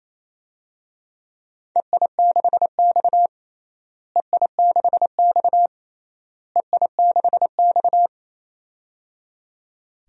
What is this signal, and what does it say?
Morse code, heard as a keyed tone, about 28 wpm, 700 Hz. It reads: EI6X EI6X EI6X
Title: DXCC-100 Call Signs - 4 Characters - Repeated 3x - 28wpm